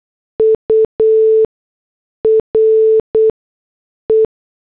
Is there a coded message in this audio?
Yes: Morse code from 0.4 to 4.3 s, heard as a keyed tone, about 8 wpm, spaced as for 6 wpm, 440 Hz. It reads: URE